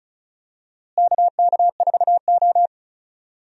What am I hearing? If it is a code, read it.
Morse code, heard as a keyed tone, about 36 wpm, 700 Hz. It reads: KK4O
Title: USA Call Signs - 4 Characters - Repeated 2x - 35wpm